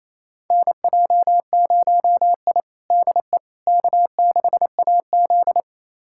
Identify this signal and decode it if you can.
Morse code, heard as a keyed tone, about 27 wpm, 700 Hz. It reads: NJ0I DE K6AZ
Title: USA Callsign DE Callsign - 4 Characters - 28wpm